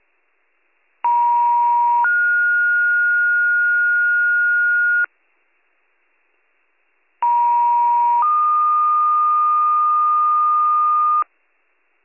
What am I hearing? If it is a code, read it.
Two-tone page, heard as a one-second tone, then a three-second tone, about 950 then 1430 Hz, about 950 then 1220 Hz, with noise. It is heard from 1.0 to 11.2 s.